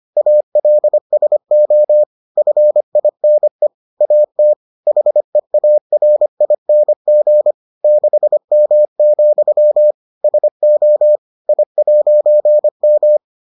Morse code, heard as a keyed tone, about 25 wpm, 600 Hz. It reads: ALSO FINE AT HEARING 6M, SO I'M